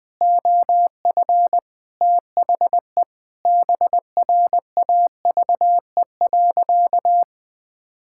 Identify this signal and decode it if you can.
Morse code, heard as a keyed tone, about 20 wpm, 700 Hz. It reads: OF THE BRAVE.
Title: Sentences with 8-Word Phrases - Spoken Beginning - 20wpm